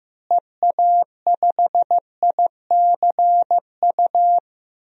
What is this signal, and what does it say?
Morse code, heard as a keyed tone, about 15 wpm, 700 Hz. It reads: EA5ICU
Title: DXCC-100 Call Signs - 6 Characters - Repeated 3x - 15wpm